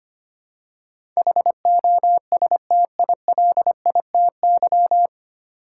Morse code, heard as a keyed tone, about 25 wpm, 700 Hz. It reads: HOSTILITY
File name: Nine Letter Words - 25wpm